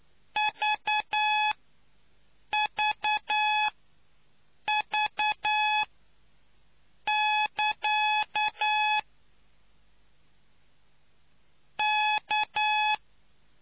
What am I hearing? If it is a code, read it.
Morse code, heard as a keyed tone, about 9 wpm, spaced as for 6 wpm, 850 Hz. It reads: VVV<KA> K